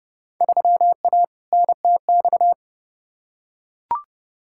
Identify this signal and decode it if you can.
Morse code, heard as a keyed tone, about 30 wpm, 700 Hz. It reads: 3A NTX
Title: Field Day Exchanges - 30wpm